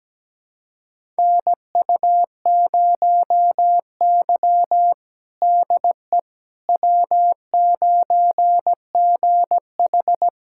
Morse code, heard as a keyed tone, about 17 wpm, 700 Hz. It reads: NU0Y DE W9GH
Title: USA Callsign DE Callsign - 4 Characters - 17wpm